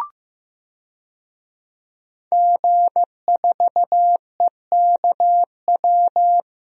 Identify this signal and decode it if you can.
Morse code, heard as a keyed tone, about 15 wpm, 700 Hz. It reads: G4EKW
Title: DXCC-100 Call Signs - 5 Characters - Repeated 3x - 15wpm